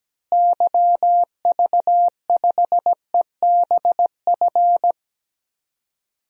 Morse code, heard as a keyed tone, about 17 wpm, 700 Hz. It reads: YV5EBF
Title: DXCC-100 Call Signs - 6 Characters - Repeated 2x - 17wpm